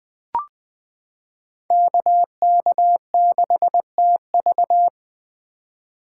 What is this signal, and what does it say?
Morse code, heard as a keyed tone, about 20 wpm, 700 Hz. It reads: KK6TV
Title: K1USN SST QSO Practice - 20wpm